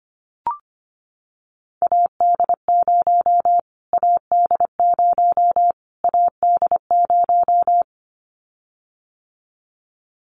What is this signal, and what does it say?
Morse code, heard as a keyed tone, about 25 wpm, 700 Hz. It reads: AD0 AD0 AD0
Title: Callsign Prefixes for Top 25 DXCC - Repeated 3x - 25wpm